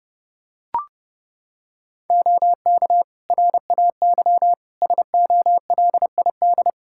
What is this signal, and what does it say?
Morse code, heard as a keyed tone, about 30 wpm, 700 Hz. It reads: OK RAY SOLID
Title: Practice QSO - Exchange 3 - More Info - 30wpm